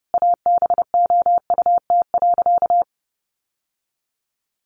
Morse code, heard as a keyed tone, about 30 wpm, 700 Hz. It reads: ABOUT.
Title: Sentences from Top 800 Words 30wpm